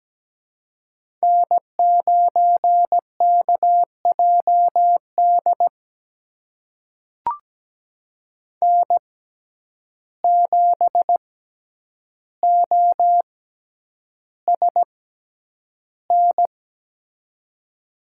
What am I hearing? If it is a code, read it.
Morse code, heard as a keyed tone, about 17 wpm, spaced as for 9 wpm, 700 Hz. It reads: N9KJD N7OSN
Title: USA Call Signs - 5 Characters - 6x Character Spacing - 17wpm